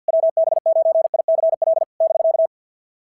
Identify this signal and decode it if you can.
Morse code, heard as a keyed tone, about 52 wpm, 690 Hz. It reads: WB9ICL <BK>
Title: SOTA QSO Practice 50wpm